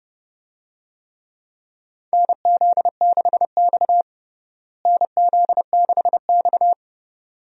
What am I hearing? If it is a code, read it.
Morse code, heard as a keyed tone, about 30 wpm, 700 Hz. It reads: NZ6X NZ6X
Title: USA Call Signs - 4 Characters - Repeated 2x - 30wpm